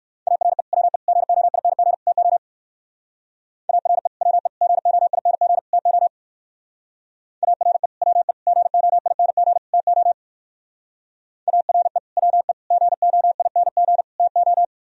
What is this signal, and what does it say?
Morse code, heard as a keyed tone, about 55 wpm, 700 Hz. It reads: ARE WE GOING TO ARE WE GOING TO ARE WE GOING TO ARE WE GOING TO
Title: Speed-Racing - Sets of 4 Words 45wpm